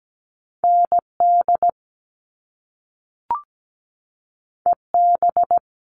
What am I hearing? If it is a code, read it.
Morse code, heard as a keyed tone, about 17 wpm, 700 Hz. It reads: ND EB
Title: All Cut Numbers - 2-Digit 17wpm